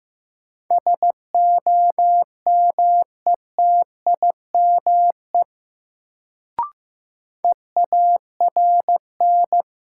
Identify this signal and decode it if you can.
Morse code, heard as a keyed tone, about 16 wpm, 700 Hz. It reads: SOMETIME EARN
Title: Sentences from Top 1500 Words - New Word Drill Alt - 15wpm